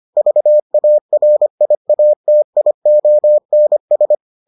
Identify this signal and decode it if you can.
Morse code, heard as a keyed tone, about 25 wpm, 600 Hz. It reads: VARIATIONS